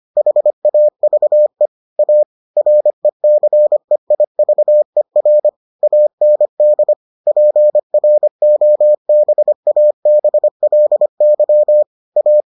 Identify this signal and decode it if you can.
Morse code, heard as a keyed tone, about 25 wpm, 600 Hz. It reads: HAVE A RECEIVER AND PROBABLY A